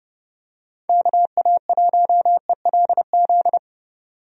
Morse code, heard as a keyed tone, about 32 wpm, 700 Hz. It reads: KA1ELZ
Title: USA Call Signs - 6 Characters - 30wpm